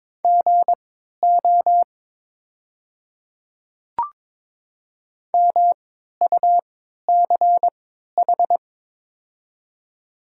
Morse code, heard as a keyed tone, about 22 wpm, spaced as for 12 wpm, 700 Hz. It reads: GO MUCH
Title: 3x Character Spacing - Top 200 Words - 22wpm